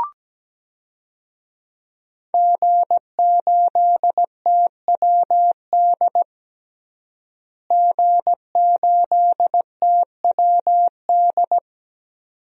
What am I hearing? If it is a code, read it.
Morse code, heard as a keyed tone, about 17 wpm, 700 Hz. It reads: G8TWD G8TWD
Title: DXCC-100 Call Signs - 5 Characters - Repeated 2x - 17wpm